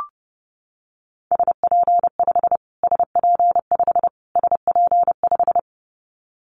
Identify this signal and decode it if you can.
Morse code, heard as a keyed tone, about 30 wpm, 700 Hz. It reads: SP5 SP5 SP5
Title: Callsign Prefixes for Top 25 DXCC - Repeated 3x - 30wpm